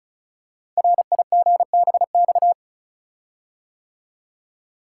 Morse code, heard as a keyed tone, about 36 wpm, 700 Hz. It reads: RIGBX
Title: Code Groups - 5 Letters - 35wpm